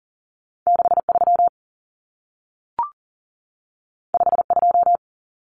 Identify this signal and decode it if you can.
Morse code, heard as a keyed tone, about 40 wpm, 700 Hz. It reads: 63 52
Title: Numbers - 2-Digit 40wpm